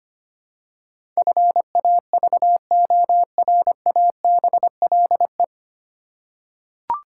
Morse code, heard as a keyed tone, about 25 wpm, 700 Hz. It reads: FAVORABLE
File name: practice / 25wpm / 9 Letter Words - First 3 Letters Given - 25wpm